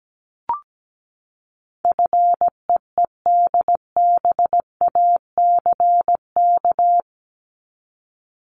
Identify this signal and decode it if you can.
Morse code, heard as a keyed tone, about 17 wpm, 700 Hz. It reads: FEEDBACK
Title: Top 1501-1750 Words 17wpm